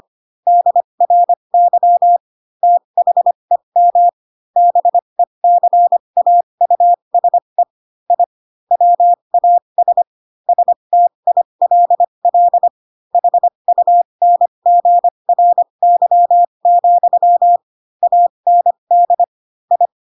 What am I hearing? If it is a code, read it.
Morse code, heard as a keyed tone, about 25 wpm, spaced as for 22 wpm, 700 Hz. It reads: DRY THEM BECAUSE I WAS STILL HUNGRY, AND I
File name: Pinocchio / Pinocchio ch7 18 25 ik1hge